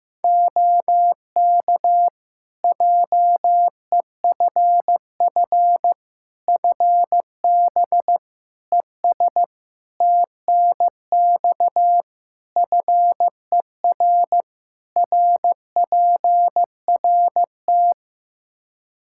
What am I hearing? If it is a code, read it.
Morse code, heard as a keyed tone, about 15 wpm, 700 Hz. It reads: OK JEFF FB ES TNX FER RPRT
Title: K1USN SST QSO Practice 15wpm